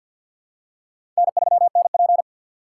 Morse code, heard as a keyed tone, about 48 wpm, 700 Hz. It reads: N3NP
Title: USA Call Signs - 4 Characters - 6x Character Spacing - 50wpm